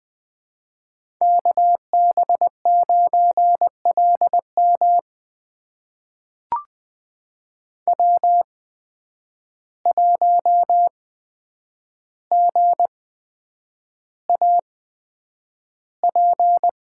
Morse code, heard as a keyed tone, about 20 wpm, spaced as for 9 wpm, 700 Hz. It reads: KB9LM W1GAP
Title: USA Call Signs - 5 Characters - 8x Character Spacing - 20wpm